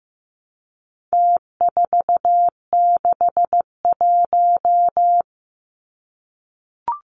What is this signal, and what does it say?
Morse code, heard as a keyed tone, about 15 wpm, 700 Hz. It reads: T461
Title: Numbers - Contest Style - 4-Digit 15wpm